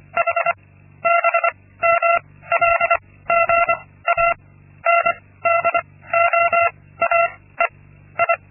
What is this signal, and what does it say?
Morse code, heard as a keyed tone, about 24 wpm, 680 Hz. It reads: H B M L G A N D O A E I
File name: Linksmakalnis-2